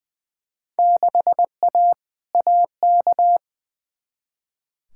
Morse code, heard as a keyed tone, about 20 wpm, 700 Hz. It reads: 6A AK
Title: Field Day Exchanges Encore - 20wpm